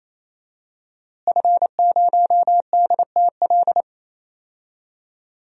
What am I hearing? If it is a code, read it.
Morse code, heard as a keyed tone, about 28 wpm, 700 Hz. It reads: F0DTL
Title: DXCC-100 Call Signs - 5 Characters - Repeated 3x - 28wpm